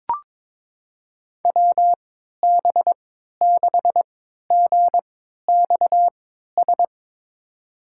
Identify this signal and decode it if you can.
Morse code, heard as a keyed tone, about 22 wpm, spaced as for 13 wpm, 700 Hz. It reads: WB6GXS
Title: USA Call Signs - 6 Characters - 3x Character Spacing - 22wpm